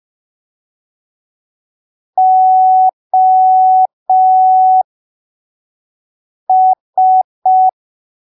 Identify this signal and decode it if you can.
Morse code, heard as a keyed tone, about 5 wpm, spaced as for 3 wpm, 750 Hz. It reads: OS
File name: arquivos / texto / texto 02 5ppm